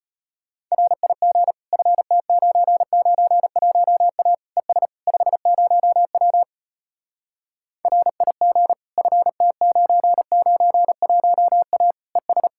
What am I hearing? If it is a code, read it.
Morse code, heard as a keyed tone, about 33 wpm, 700 Hz. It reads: RIG FT991A ES 50W RIG FT991A ES